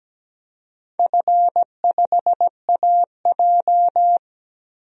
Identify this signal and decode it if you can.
Morse code, heard as a keyed tone, about 17 wpm, 700 Hz. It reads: F5AJ
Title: DXCC-100 Call Signs - 4 Characters - 17wpm